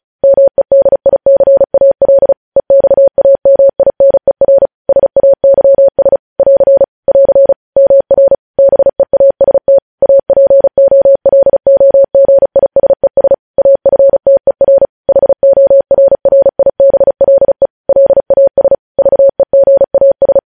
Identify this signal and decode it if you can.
Morse code, heard as a keyed tone, about 36 wpm, 550 Hz. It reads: MEDICAL EXAMINER SAYS <AR> <AR> MR BEAST APOLOGISES AFTER HORRIBLE LAS VEGAS
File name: podcast / News headlines-1x-35x35